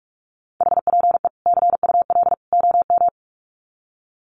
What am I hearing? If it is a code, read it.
Morse code, heard as a keyed tone, about 44 wpm, 700 Hz. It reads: HPE CUL OM
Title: Common QSO Phrases 45wpm